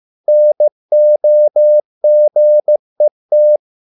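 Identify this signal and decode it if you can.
Morse code, heard as a keyed tone, about 15 wpm, 600 Hz. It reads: NOGET
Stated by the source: Morse code at a slow 15 wpm from CW med Gnister.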